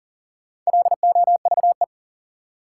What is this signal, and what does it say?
Morse code, heard as a keyed tone, about 40 wpm, 700 Hz. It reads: LOVE
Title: Sets of 5 Words Encore - New Word Drill - 40wpm